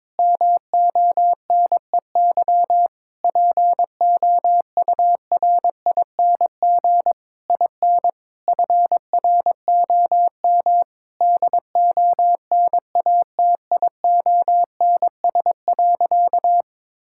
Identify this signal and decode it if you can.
Morse code, heard as a keyed tone, about 22 wpm, 700 Hz. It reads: MONEY POURING IN FROM DONATIONS.
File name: Sentences with 9-Word Phrases - 22wpm